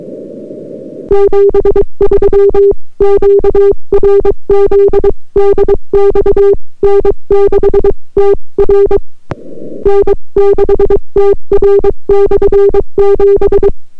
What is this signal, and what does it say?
Morse code, heard as a keyed tone, about 23 wpm, 400 Hz. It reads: 73 QRZ DX N6TR N6TR/7